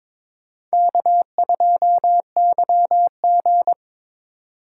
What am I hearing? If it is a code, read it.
Morse code, heard as a keyed tone, about 22 wpm, 700 Hz. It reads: K2YG
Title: USA Call Signs - 2x Character Spacing - 22wpm